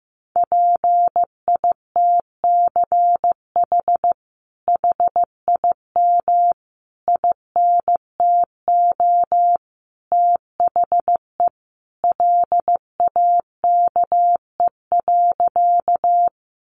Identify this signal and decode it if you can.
Morse code, heard as a keyed tone, about 15 wpm, 700 Hz. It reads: PITCH HIM INTO THE LAKE.
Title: Sentences from Top 1300 Words 15wpm